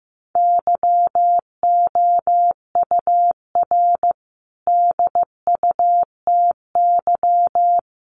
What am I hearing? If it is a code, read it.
Morse code, heard as a keyed tone, about 15 wpm, 700 Hz. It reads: YOUR DUTY
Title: Sentences from Top 1500 Words 15wpm